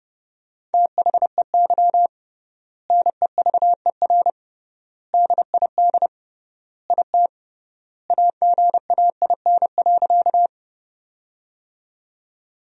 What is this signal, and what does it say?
Morse code, heard as a keyed tone, about 30 wpm, 700 Hz. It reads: THEY NEVER DID IT AGAIN.
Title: 3x Word-Spacing - Sentences from Top 300 Words - 30wpm